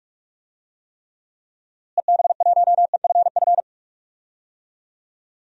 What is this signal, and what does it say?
Morse code, heard as a keyed tone, about 44 wpm, 700 Hz. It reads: EB1EFF
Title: DXCC-100 Call Signs - 6 Characters - Repeated 3x - 45wpm